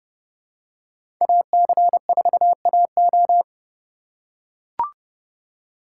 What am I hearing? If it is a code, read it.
Morse code, heard as a keyed tone, about 29 wpm, 700 Hz. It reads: AC4AO
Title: USA Call Signs - 5 Characters - 3x Character Spacing - 30wpm